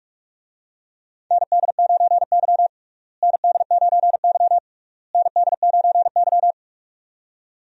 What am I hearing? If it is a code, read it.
Morse code, heard as a keyed tone, about 44 wpm, 700 Hz. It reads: ND9Y ND9Y ND9Y
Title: USA Call Signs - 4 Characters - Repeated 3x - 45wpm